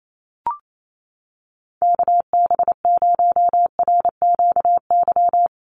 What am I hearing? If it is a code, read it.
Morse code, heard as a keyed tone, about 28 wpm, 700 Hz. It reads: KB0RQY
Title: USA Call Signs 28wpm